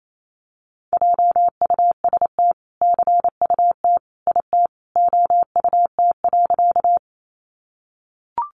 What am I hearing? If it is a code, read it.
Morse code, heard as a keyed tone, about 28 wpm, 700 Hz. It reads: JUST CUT IT OUT.
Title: Sentences from Top 300 Words 28wpm